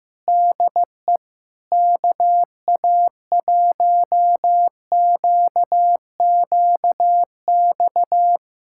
Morse code, heard as a keyed tone, about 15 wpm, 700 Hz. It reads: DE KA1QQX
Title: USA Callsign DE Callsign - 6 Characters - 15wpm